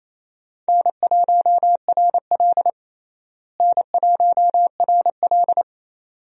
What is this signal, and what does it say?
Morse code, heard as a keyed tone, about 28 wpm, 700 Hz. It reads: N1RL N1RL
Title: USA Call Signs - 4 Characters - Repeated 2x - 28wpm